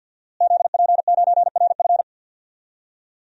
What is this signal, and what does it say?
Morse code, heard as a keyed tone, about 52 wpm, 700 Hz. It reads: ZP9RF
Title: DXCC-100 Call Signs - 5 Characters - Repeated 3x - 50wpm